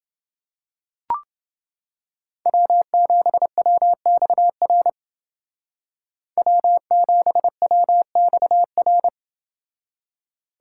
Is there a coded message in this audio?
Yes: Morse code from 2.5 to 9.1 s, heard as a keyed tone, about 29 wpm, 700 Hz. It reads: W7WXR W7WXR